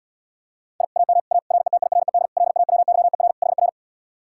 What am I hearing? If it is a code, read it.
Morse code, heard as a keyed tone, about 80 wpm, 700 Hz. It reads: I AM A RETIRED LAWYER HR